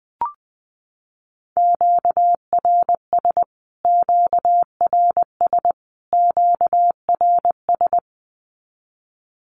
Morse code, heard as a keyed tone, about 20 wpm, 700 Hz. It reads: QRS QRS QRS
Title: Q-Codes Repeated 3x 20wpm